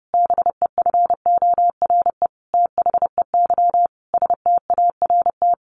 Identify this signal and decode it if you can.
Morse code, heard as a keyed tone, about 30 wpm, 700 Hz. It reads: BEFORE THEY START
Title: Sentences from Top 1100 Words 30wpm